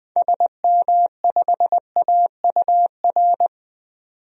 Morse code, heard as a keyed tone, about 21 wpm, 700 Hz. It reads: SM5AUR
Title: DXCC-100 Call Signs - 6 Characters - 20wpm